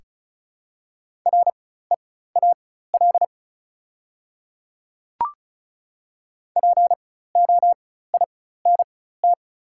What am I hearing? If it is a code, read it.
Morse code, heard as a keyed tone, about 36 wpm, spaced as for 16 wpm, 700 Hz. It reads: REAL POINT
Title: 4x Character Spacing - Top 500 Words - 35wpm